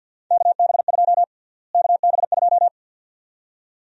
Morse code, heard as a keyed tone, about 52 wpm, 700 Hz. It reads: KB2 KB2
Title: USA Call Sign Prefixes - Repeated 2x - 50wpm